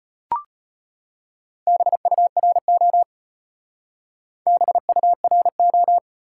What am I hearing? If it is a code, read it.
Morse code, heard as a keyed tone, about 35 wpm, 700 Hz. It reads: BURO BURO